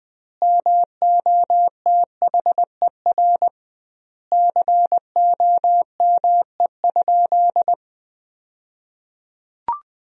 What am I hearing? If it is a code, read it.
Morse code, heard as a keyed tone, about 21 wpm, 700 Hz. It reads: MOTHER COME?
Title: 2x Word-Spacing - Sentences from Top 200 Words - 20wpm